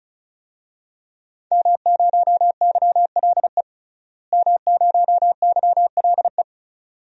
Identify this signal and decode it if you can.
Morse code, heard as a keyed tone, about 36 wpm, 700 Hz. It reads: M0YLE M0YLE
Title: DXCC-100 Call Signs - 5 Characters - Repeated 2x - 35wpm